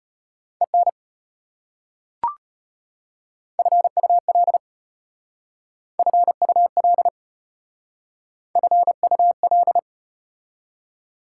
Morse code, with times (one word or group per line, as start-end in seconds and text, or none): 0.6-0.9 EN
3.6-4.6 FUL
6.0-7.1 FUL
8.6-9.8 FUL